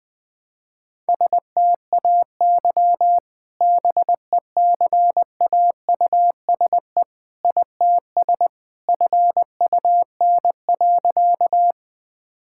Morse code, heard as a keyed tone, about 20 wpm, 700 Hz. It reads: STAY BECAUSE ITS FUN.
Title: Sentences with 2-Word Phrases - 20wpm